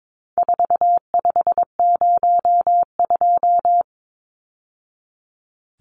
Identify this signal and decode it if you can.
Morse code, heard as a keyed tone, about 22 wpm, 700 Hz. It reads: 4502
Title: Numbers - 4-Digit 22wpm